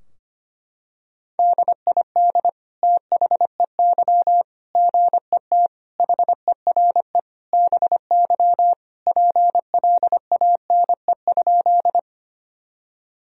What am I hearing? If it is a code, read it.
Morse code, heard as a keyed tone, about 25 wpm, 700 Hz. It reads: DID THEY GET HERE BY PLANE?
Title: Sentences from Top 1000 Words - Review Part 3 - 25wpm